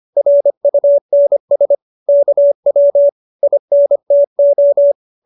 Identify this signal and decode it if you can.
Morse code, heard as a keyed tone, about 25 wpm, 570 Hz. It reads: RUNS KW INTO